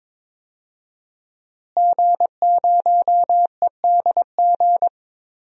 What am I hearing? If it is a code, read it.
Morse code, heard as a keyed tone, about 22 wpm, 700 Hz. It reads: G0EDG
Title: DXCC-100 Call Signs - 5 Characters - Repeated 2x - 22wpm